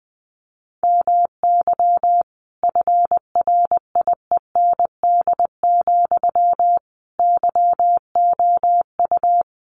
Morse code, heard as a keyed tone, about 20 wpm, 700 Hz. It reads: MY FRIEND, YOU
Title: Sentences from Top 1300 Words 20wpm